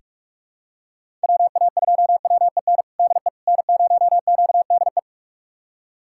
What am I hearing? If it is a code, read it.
Morse code, heard as a keyed tone, about 46 wpm, 700 Hz. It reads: WA2WEN DE N0QDE
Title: USA Callsign DE Callsign 45wpm